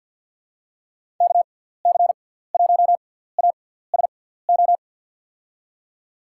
Morse code, heard as a keyed tone, about 52 wpm, spaced as for 17 wpm, 700 Hz. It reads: KC1ASO
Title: USA Call Signs - 6x Character Spacing - 50wpm